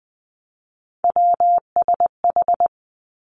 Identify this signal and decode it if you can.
Morse code, heard as a keyed tone, about 20 wpm, 700 Hz. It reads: WSH
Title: Three Challenging Letter-Number Combinations 20wpm